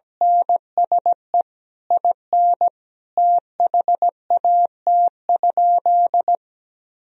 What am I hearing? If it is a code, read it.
Morse code, heard as a keyed tone, about 17 wpm, 700 Hz. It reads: NSE IN THAT?
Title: Sentences from Top 1000 Words - Review Part 4 - 17wpm